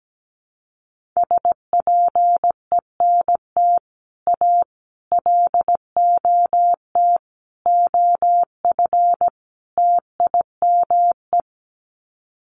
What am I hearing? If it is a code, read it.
Morse code, heard as a keyed tone, about 17 wpm, 700 Hz. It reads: SPENT A LOT OF TIME
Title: Sets of 5 Words 17wpm